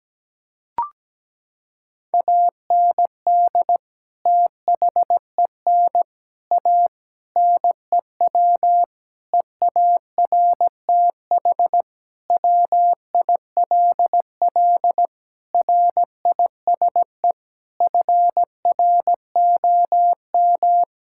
Morse code, heard as a keyed tone, about 17 wpm, 700 Hz. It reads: AND THEN A NEW EARTH WILL RISE FROM
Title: Sentences from Top 1000 Words - Review Part 6 - 17wpm